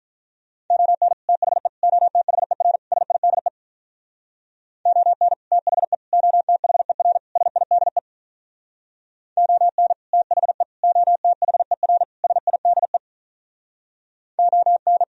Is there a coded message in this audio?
Yes: Morse code from 0.7 to 15.0 s, heard as a keyed tone, about 45 wpm, 700 Hz. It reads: ON THE OTHER SIDE ON THE OTHER SIDE ON THE OTHER SIDE ON